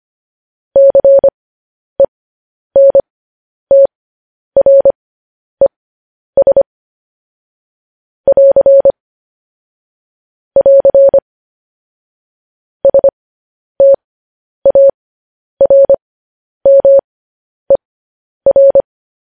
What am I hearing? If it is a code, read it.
Morse code, heard as a keyed tone, about 25 wpm, spaced as for 10 wpm, 550 Hz. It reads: CENTRES <AR> <AR> STARMER
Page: news-headlines-in-morse-code-at-10-wpm